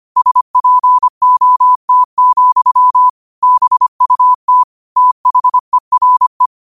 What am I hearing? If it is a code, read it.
Morse code, heard as a keyed tone, about 25 wpm, 1000 Hz. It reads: SPOT, BUT THERE
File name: qotdfiles / 25wpm